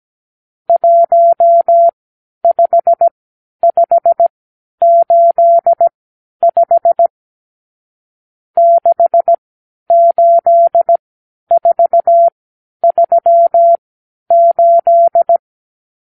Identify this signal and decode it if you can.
Morse code, heard as a keyed tone, about 17 wpm, spaced as for 10 wpm, 680 Hz. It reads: 15585 68438
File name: k7qo / Lesson 54 (9)